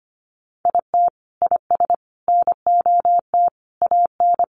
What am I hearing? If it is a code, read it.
Morse code, heard as a keyed tone, about 25 wpm, 700 Hz. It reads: IT IS NOT AN
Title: Sentences from Top 1500 Words 25wpm